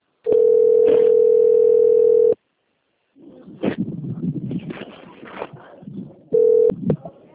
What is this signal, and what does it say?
Ringback tone (North America), 440 + 480 Hz.